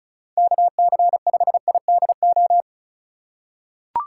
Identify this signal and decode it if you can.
Morse code, heard as a keyed tone, about 36 wpm, 700 Hz. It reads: KC5IDO
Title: USA Call Signs - 6 Characters - 3x Character Spacing - 35wpm